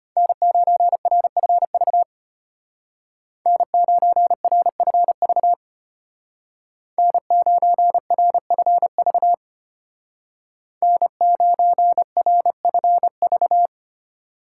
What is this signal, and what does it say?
Morse code, heard as a keyed tone, about 32 wpm, 700 Hz. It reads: N9RFV N9RFV N9RFV N9RFV